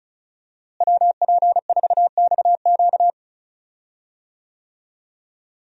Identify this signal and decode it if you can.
Morse code, heard as a keyed tone, about 36 wpm, 700 Hz. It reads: WP4XQ
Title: USA Call Signs - 5 Characters - Repeated 2x - 35wpm